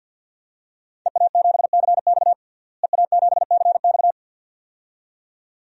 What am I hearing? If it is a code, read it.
Morse code, heard as a keyed tone, about 51 wpm, 700 Hz. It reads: EA7CX EA7CX
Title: DXCC-100 Call Signs - 5 Characters - Repeated 2x - 50wpm